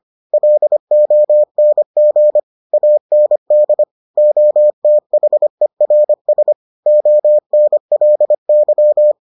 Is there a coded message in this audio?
Yes: Morse code from 0.3 to 9.1 s, heard as a keyed tone, about 25 wpm, 600 Hz. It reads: LONG AND OTHERS ONLY